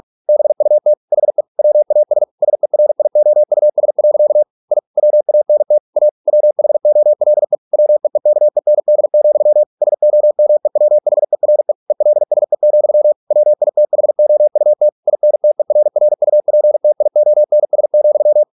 Morse code, heard as a keyed tone, about 46 wpm, 600 Hz. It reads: BUT HE WAS SERIOUS. I WANT A WHOLE WEEKEND, SOMEWHERE ELSE, WITHOUT INTERRUPTIONS,